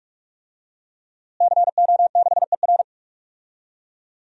Morse code, heard as a keyed tone, about 46 wpm, 700 Hz. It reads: CO6ER
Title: DXCC-100 Call Signs - 5 Characters - Repeated 3x - 45wpm